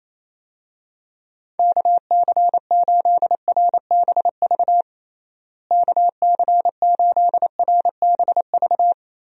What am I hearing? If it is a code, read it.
Morse code, heard as a keyed tone, about 27 wpm, 700 Hz. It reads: KC8RBV KC8RBV
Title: USA Call Signs - 6 Characters - Repeated 2x - 28wpm